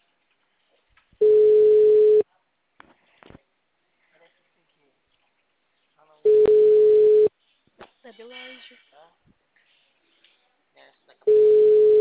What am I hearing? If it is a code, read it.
Ringback tone (Europe, 425 Hz), with speech.